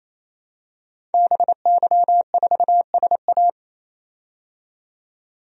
Morse code, heard as a keyed tone, about 28 wpm, 700 Hz. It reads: BY4SA
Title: DXCC-100 Call Signs - 5 Characters - 28wpm